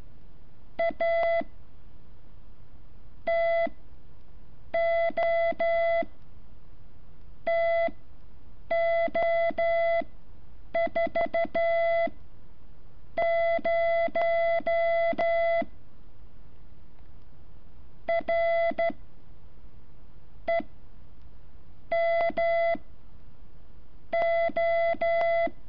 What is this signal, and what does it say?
Morse code, heard as a keyed tone, about 13 wpm, 670 Hz. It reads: A T O T O 4 0 R E M O
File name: curso / ejercicio 11